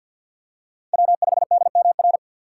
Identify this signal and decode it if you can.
Morse code, heard as a keyed tone, about 53 wpm, 700 Hz. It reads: W5DMR
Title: USA Call Signs - 2x Character Spacing - 50wpm